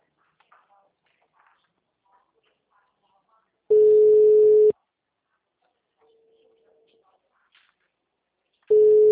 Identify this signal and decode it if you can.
Ringback tone (Europe, 425 Hz).